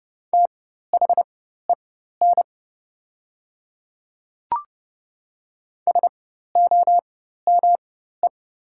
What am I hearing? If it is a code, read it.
Morse code, heard as a keyed tone, about 31 wpm, spaced as for 14 wpm, 700 Hz. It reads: THEN SOME
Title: 4x Character Spacing - Top 100 Words - 30wpm